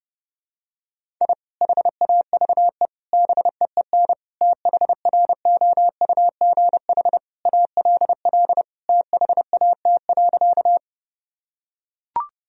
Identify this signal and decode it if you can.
Morse code, heard as a keyed tone, about 30 wpm, 700 Hz. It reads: I HAVE BEEN THROUGH ALL THAT.
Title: Sentences from Top 1000 Words - Review Part 1 - 30wpm